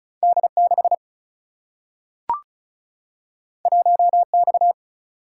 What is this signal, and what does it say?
Morse code, heard as a keyed tone, about 36 wpm, 700 Hz. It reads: D 6 1 X
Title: Single-Letter Single-Number 35wpm